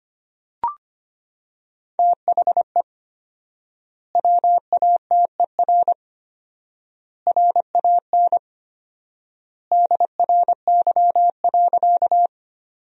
Morse code, heard as a keyed tone, about 25 wpm, 700 Hz. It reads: THE WATER RAN DRY.
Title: 4x Word-Spacing - Sentences from Top 500 Words - 25wpm